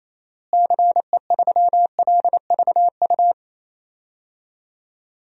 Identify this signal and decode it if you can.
Morse code, heard as a keyed tone, about 27 wpm, 700 Hz. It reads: CE3LVU
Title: DXCC-100 Call Signs - 6 Characters - Repeated 2x - 28wpm